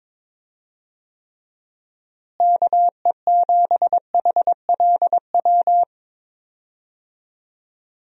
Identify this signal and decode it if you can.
Morse code, heard as a keyed tone, about 22 wpm, 700 Hz. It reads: KE7HLW
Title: USA Call Signs - 6 Characters - Repeated 2x - 22wpm